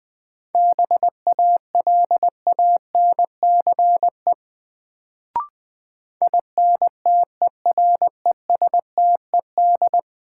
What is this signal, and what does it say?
Morse code, heard as a keyed tone, about 20 wpm, 700 Hz. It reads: BALANCE INTERESTED
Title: Sentences from Top 900 Words - New Word Drill - 20wpm